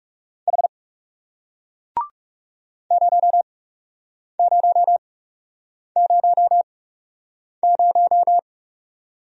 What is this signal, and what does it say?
Morse code, heard as a keyed tone, about 39 wpm, spaced as for 8 wpm, 700 Hz. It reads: H 0000